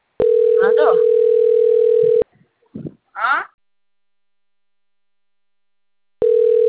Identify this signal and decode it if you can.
Ringback tone (North America), 440 + 480 Hz, with speech.
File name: out- - - - - .91394